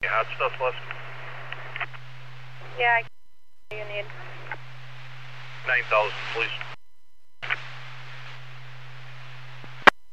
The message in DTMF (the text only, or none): none